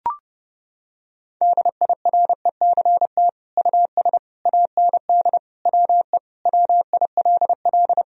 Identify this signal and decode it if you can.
Morse code, heard as a keyed tone, about 32 wpm, 700 Hz. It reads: DIRECT US AND WE WILL
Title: Sentences from Top 1000 Words - Review Part 2 - 30wpm